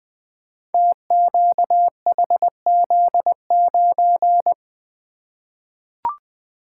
Morse code, heard as a keyed tone, about 20 wpm, 700 Hz. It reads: TQHZ9
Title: Code Groups - 5 Letter:Numbers - 20wpm